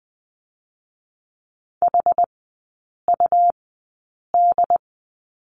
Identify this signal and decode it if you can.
Morse code, heard as a keyed tone, about 20 wpm, spaced as for 8 wpm, 690 Hz. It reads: HUD
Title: 205 Lesson 33e - Letter j - Three Characters 20wpm